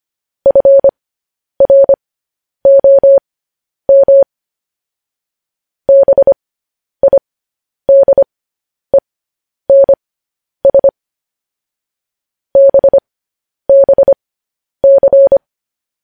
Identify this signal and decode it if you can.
Morse code, heard as a keyed tone, about 25 wpm, spaced as for 10 wpm, 550 Hz. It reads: FROM BIDENS BBC
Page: news-headlines-in-morse-code-at-10-wpm